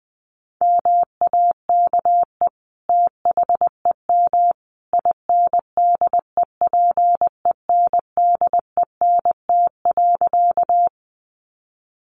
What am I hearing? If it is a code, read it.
Morse code, heard as a keyed tone, about 20 wpm, 700 Hz. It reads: MAKE THEM INDEPENDENT.
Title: Sentences from Top 900 Words 20wpm